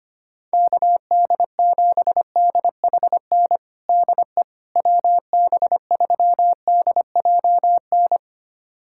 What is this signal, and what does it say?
Morse code, heard as a keyed tone, about 25 wpm, 700 Hz. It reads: KD7DHN DE WB3DJN
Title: USA Callsign DE Callsign 25wpm